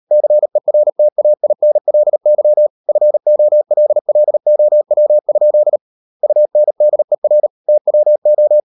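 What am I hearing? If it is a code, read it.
Morse code, heard as a keyed tone, about 38 wpm, 600 Hz. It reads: CERTAINLY FOLLOW? UNDER TWO
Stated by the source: Morse code practice at 35–39 wpm sending English text of Ken Follett's novel Never, from CW med Gnister.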